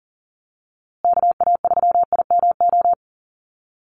Morse code, heard as a keyed tone, about 40 wpm, 700 Hz. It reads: KA3IMO
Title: USA Call Signs Repeated 3x 40wpm